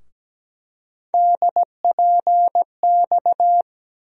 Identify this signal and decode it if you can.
Morse code, heard as a keyed tone, about 17 wpm, 700 Hz. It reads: DPX
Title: USA Call Sign Postfixes - 3 Letters 17wpm